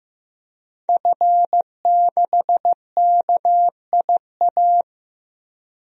Morse code, heard as a keyed tone, about 15 wpm, 700 Hz. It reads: F6KIA
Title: DXCC-100 Call Signs - 5 Characters - Repeated 3x - 15wpm